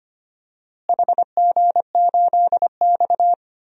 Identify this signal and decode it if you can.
Morse code, heard as a keyed tone, about 25 wpm, 700 Hz. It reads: HG8X
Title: DXCC-100 Call Signs - 4 Characters - Repeated 3x - 25wpm